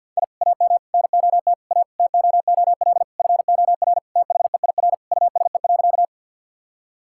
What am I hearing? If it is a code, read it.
Morse code, heard as a keyed tone, about 50 wpm, 700 Hz. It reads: I AM NOT A TOOL FOR THEIR USE.
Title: Sentences from Top 1000 Words - Review Part 6 - 50wpm